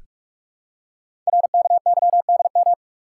Morse code, heard as a keyed tone, about 44 wpm, 700 Hz. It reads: RKYDM
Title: Code Groups - 5 Letters - 45wpm